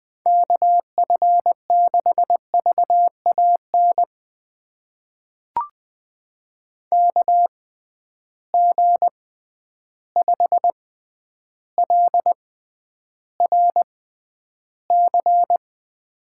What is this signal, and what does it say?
Morse code, heard as a keyed tone, about 20 wpm, spaced as for 10 wpm, 700 Hz. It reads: KF6VAN KG5LRC
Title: USA Call Signs - 6 Characters - 6x Character Spacing - 20wpm